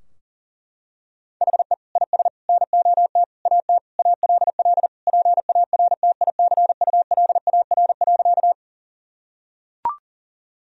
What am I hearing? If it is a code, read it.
Morse code, heard as a keyed tone, about 37 wpm, 700 Hz. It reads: HE IS NOT AT ALL PARTICULAR.
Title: Sentences from Top 1000 Words - Review Part 4 - 40wpm